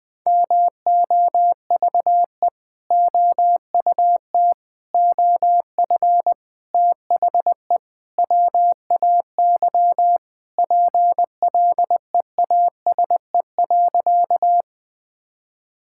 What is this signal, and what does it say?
Morse code, heard as a keyed tone, about 20 wpm, 700 Hz. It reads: MOVE OUT OF THE WAY PLEASE.
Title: Sentences with 4-Word Phrases - 20wpm